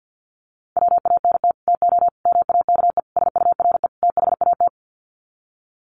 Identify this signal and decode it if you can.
Morse code, heard as a keyed tone, about 50 wpm, 700 Hz. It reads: WANT TO MAKE SURE THAT
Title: Sets of 5 Words 50wpm